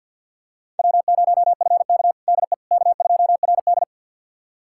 Morse code, heard as a keyed tone, about 53 wpm, 700 Hz. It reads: W0FK DE K2RD
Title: USA Callsign DE Callsign - 4 Characters - 50wpm